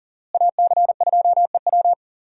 Morse code, heard as a keyed tone, about 40 wpm, 700 Hz. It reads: AC2EW
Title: USA Call Signs - 5 Characters - Repeated 3x - 40wpm